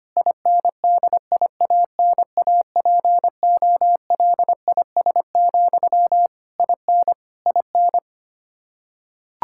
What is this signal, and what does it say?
Morse code, heard as a keyed tone, about 25 wpm, 700 Hz. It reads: INDIANAPOLIS, IN IN
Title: Practice QSO - Exchange 1 - Beginning - 25wpm